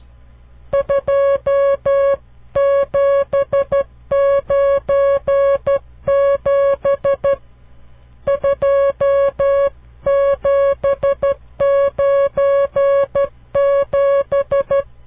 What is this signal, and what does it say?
Morse code, heard as a keyed tone, about 12 wpm, 560 Hz. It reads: 2797 2797
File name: CW Learning Programs / farnsworth / unit 5